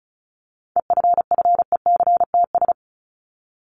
Morse code, heard as a keyed tone, about 34 wpm, 700 Hz. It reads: EFFECTS
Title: ES:S Suffix - Top 100 Words with Base Words 35wpm